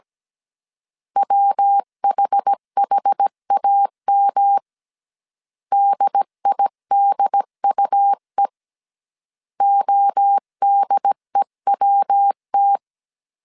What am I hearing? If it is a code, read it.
Morse code, heard as a keyed tone, about 17 wpm, 800 Hz. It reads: WHHAM DIDUE ODEWT